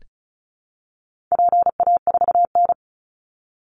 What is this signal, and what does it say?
Morse code, heard as a keyed tone, about 34 wpm, 700 Hz. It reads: PA4N
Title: Call Signs Repeated 2x - 35wpm